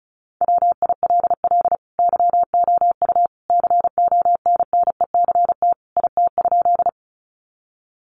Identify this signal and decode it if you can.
Morse code, heard as a keyed tone, about 34 wpm, 700 Hz. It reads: WILL YOU CONNECT IT?
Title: Sentences from Top 1500 Words 35wpm